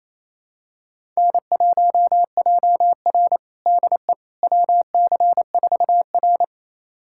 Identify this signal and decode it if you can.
Morse code, heard as a keyed tone, about 28 wpm, 700 Hz. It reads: N1JR DE WC4R
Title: USA Callsign DE Callsign - 4 Characters - 28wpm